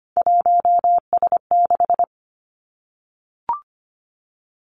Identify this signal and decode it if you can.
Morse code, heard as a keyed tone, about 25 wpm, 700 Hz. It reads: 1S6
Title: Three Challenging Letter-Number Combinations 25wpm